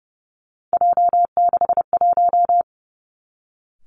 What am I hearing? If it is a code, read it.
Morse code, heard as a keyed tone, about 30 wpm, 700 Hz. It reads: J61
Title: Three Challenging Letter-Number Combinations 30wpm